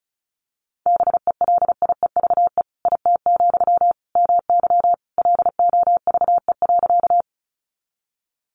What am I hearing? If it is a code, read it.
Morse code, heard as a keyed tone, about 34 wpm, 700 Hz. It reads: BELIEVE IT, MY LOVE.
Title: Sentences from Top 1000 Words 35wpm